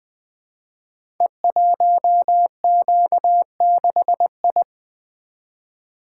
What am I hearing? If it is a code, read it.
Morse code, heard as a keyed tone, about 20 wpm, 700 Hz. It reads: E1Q6I
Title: Code Groups - 5 Letter:Numbers - 20wpm